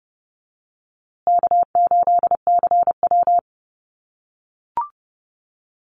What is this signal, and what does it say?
Morse code, heard as a keyed tone, about 30 wpm, 700 Hz. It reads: K8CW
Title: USA Call Signs Repeated 3x 30wpm